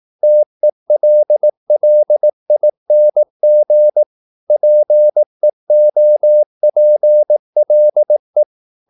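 Morse code, heard as a keyed tone, about 18 wpm, 600 Hz. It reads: TELLING PEOPLE
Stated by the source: Morse practice at 18 wpm, English text, from CW med Gnister.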